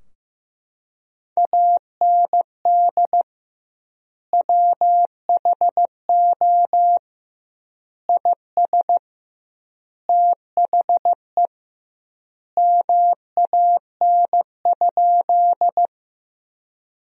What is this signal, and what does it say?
Morse code, heard as a keyed tone, about 15 wpm, 700 Hz. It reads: AND WHO IS THE MAN?
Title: 2x Word-Spacing - Sentences from Top 200 Words - 15wpm